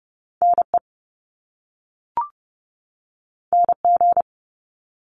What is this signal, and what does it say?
Morse code, heard as a keyed tone, about 30 wpm, 700 Hz. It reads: NE NG
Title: Top Two Letter Combinations in Words 30wpm